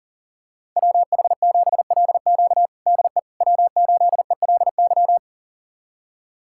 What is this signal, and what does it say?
Morse code, heard as a keyed tone, about 40 wpm, 700 Hz. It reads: WH7LQ DE W8ELY
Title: USA Callsign DE Callsign - 5 Characters - 40wpm